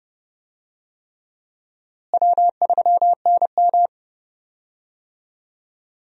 Morse code, heard as a keyed tone, about 31 wpm, 700 Hz. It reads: W3NM
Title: USA Call Signs - 4 Characters - Repeated 3x - 30wpm